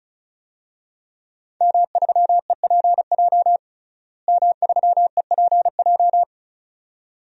Morse code, heard as a keyed tone, about 34 wpm, 700 Hz. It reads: M3EPJ M3EPJ
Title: DXCC-100 Call Signs - 5 Characters - Repeated 2x - 35wpm